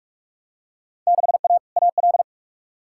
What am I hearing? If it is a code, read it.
Morse code, heard as a keyed tone, about 42 wpm, 700 Hz. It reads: 6A AL
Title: Field Day Exchanges Encore - 45wpm